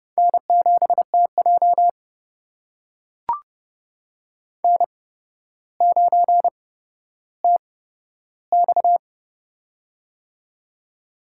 Morse code, heard as a keyed tone, about 30 wpm, spaced as for 13 wpm, 700 Hz. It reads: N7TJ N9TX
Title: USA Call Signs - 4 Characters - 8x Character Spacing - 30wpm